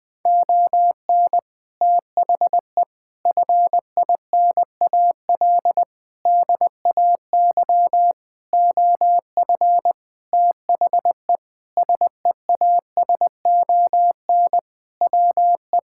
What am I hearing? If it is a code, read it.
Morse code, heard as a keyed tone, about 20 wpm, 700 Hz. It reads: ON THE FINAL DAY OF THE SEASON WE
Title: Sentences with 7-Word Phrases - Spoken Beginning - 20wpm